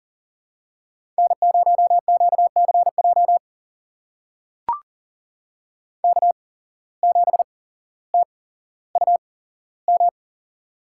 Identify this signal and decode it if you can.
Morse code, heard as a keyed tone, about 39 wpm, spaced as for 17 wpm, 700 Hz. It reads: N0QCJ K7TUM
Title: USA Call Signs - 5 Characters - 8x Character Spacing - 40wpm